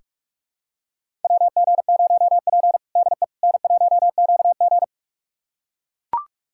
Morse code, heard as a keyed tone, about 43 wpm, 700 Hz. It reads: WG0P DE N1QG
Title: USA Callsign DE Callsign - 4 Characters - 45wpm